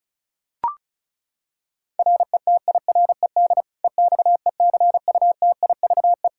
Morse code, heard as a keyed tone, about 36 wpm, 700 Hz. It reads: RETIRED EXECUTIVE
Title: Practice QSO - Exchange 3 - More Info - 35wpm